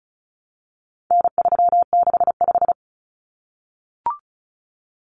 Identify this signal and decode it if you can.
Morse code, heard as a keyed tone, about 34 wpm, 700 Hz. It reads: N365
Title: Common Cut Numbers - 4-Digit 35wpm